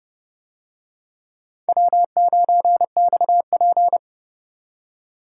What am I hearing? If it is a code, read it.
Morse code, heard as a keyed tone, about 28 wpm, 700 Hz. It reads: W9XP